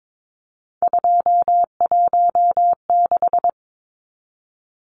Morse code, heard as a keyed tone, about 22 wpm, 700 Hz. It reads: 216
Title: Numbers - 3-Digit 22wpm